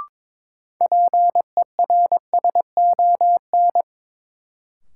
Morse code, heard as a keyed tone, about 22 wpm, 700 Hz. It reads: PERSON
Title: Sentences from Top 400 Words - New Word Drill - 22wpm